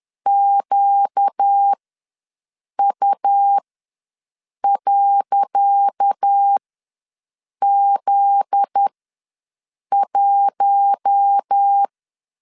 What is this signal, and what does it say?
Morse code, heard as a keyed tone, about 10 wpm, spaced as for 6 wpm, 800 Hz. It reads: QU.Z1
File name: LEZIONI / 10 SIGNS / 10C DOT ws 10wpm 4213